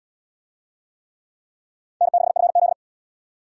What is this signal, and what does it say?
Morse code, heard as a keyed tone, about 74 wpm, 700 Hz. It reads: N6FZ